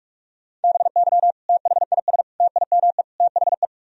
Morse code, heard as a keyed tone, about 48 wpm, 700 Hz. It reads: BY THIS TIME THE
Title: Sentences from Top 1000 Words - Review Part 1 - 45wpm